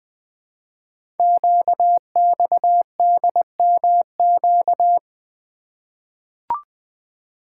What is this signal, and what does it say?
Morse code, heard as a keyed tone, about 20 wpm, 700 Hz. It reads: QXDMQ
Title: Code Groups - 5 Letters - 20wpm